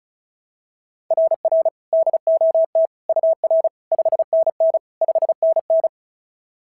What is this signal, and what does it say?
Morse code, heard as a keyed tone, about 36 wpm, 690 Hz. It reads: RR DOT UR 5NN 5NN
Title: SOTA QSO Practice 35wpm